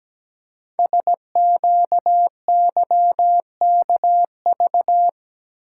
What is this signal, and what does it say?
Morse code, heard as a keyed tone, about 17 wpm, 700 Hz. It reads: SQYKV
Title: Code Groups - 5 Letters - 17wpm